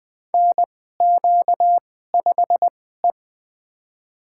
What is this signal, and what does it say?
Morse code, heard as a keyed tone, about 21 wpm, spaced as for 15 wpm, 700 Hz. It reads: NQ5E
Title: USA Call Signs - 4 Characters - 2x Character Spacing - 20wpm